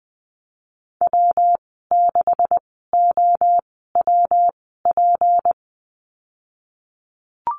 Morse code, heard as a keyed tone, about 20 wpm, spaced as for 15 wpm, 700 Hz. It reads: W6OWP
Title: 161 Lesson 26g - Letter g - Call Signs 20wpm